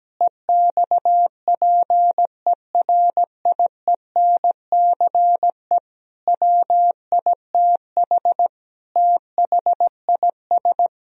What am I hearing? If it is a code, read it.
Morse code, heard as a keyed tone, about 17 wpm, 700 Hz. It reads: EXPERIENCE WITH THIS
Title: Sentences from Top 1000 Words - Review Part 3 - 17wpm